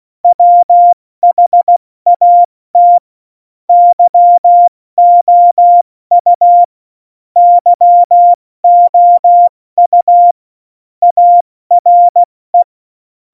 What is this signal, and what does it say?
Morse code, heard as a keyed tone, about 16 wpm, spaced as for 14 wpm, 700 Hz. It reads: WHAT YOU YOU ARE
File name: Morsekurs / Witze / Witz 25 Juli 17, Tempo 70 F80